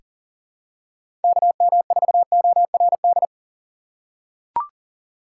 Morse code, heard as a keyed tone, about 40 wpm, 700 Hz. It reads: KM4ORD
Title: USA Call Signs - 4x Character Spacing - 40wpm